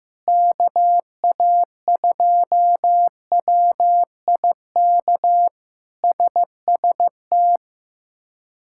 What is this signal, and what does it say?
Morse code, heard as a keyed tone, about 15 wpm, 700 Hz. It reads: KA2WIK SST
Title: K1USN SST QSO Practice - 15wpm